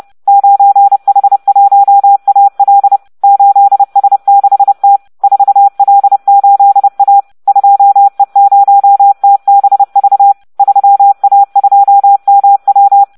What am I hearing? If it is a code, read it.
Morse code, heard as a keyed tone, about 30 wpm, 800 Hz. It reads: 9H1AL 8S6T 4L8A 2E0TBV 3A2MW